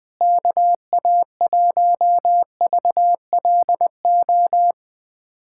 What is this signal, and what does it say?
Morse code, heard as a keyed tone, about 20 wpm, 700 Hz. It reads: KA1VLO
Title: USA Callsign DE Callsign - 6 Characters - 20wpm